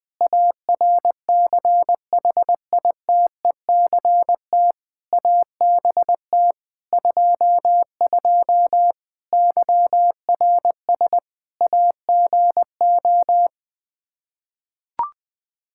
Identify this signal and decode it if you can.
Morse code, heard as a keyed tone, about 20 wpm, 700 Hz. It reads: ARCHITECT ABT 22 YRS AGO
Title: Practice QSO - Exchange 3 - More Info - 20wpm